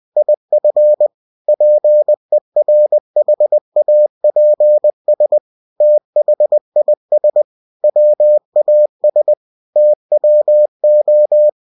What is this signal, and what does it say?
Morse code, heard as a keyed tone, about 20 wpm, 600 Hz. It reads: IF PERHAPS THIS WAS TWO
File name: cw-books / winnie-the-pooh / winnie-20wpm-page-0022